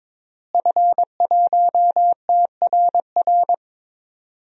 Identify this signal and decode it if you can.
Morse code, heard as a keyed tone, about 22 wpm, 700 Hz. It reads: F1TRR
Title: DXCC-100 Call Signs - 5 Characters - Repeated 3x - 22wpm